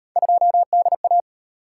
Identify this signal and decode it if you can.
Morse code, heard as a keyed tone, about 35 wpm, 700 Hz. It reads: 2DA